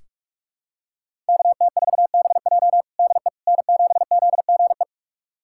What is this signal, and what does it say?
Morse code, heard as a keyed tone, about 45 wpm, 700 Hz. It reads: KT4BJ DE N7ZGE
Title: USA Callsign DE Callsign - 5 Characters - 45wpm